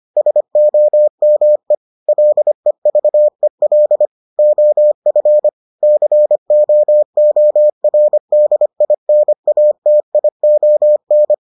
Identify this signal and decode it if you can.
Morse code, heard as a keyed tone, about 25 wpm, 600 Hz. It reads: SOME LEVEL OF COORDINATION